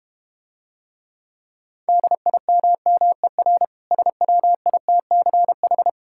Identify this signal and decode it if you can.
Morse code, heard as a keyed tone, about 31 wpm, 700 Hz. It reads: DIMMER SWITCH